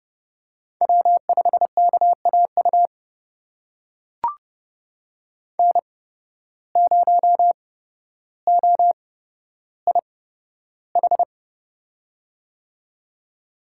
Morse code, heard as a keyed tone, about 30 wpm, spaced as for 13 wpm, 700 Hz. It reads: W5KAU N0OIH
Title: USA Call Signs - 5 Characters - 8x Character Spacing - 30wpm